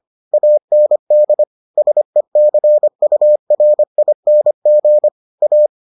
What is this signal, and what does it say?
Morse code, heard as a keyed tone, about 25 wpm, 600 Hz. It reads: AND SECURING A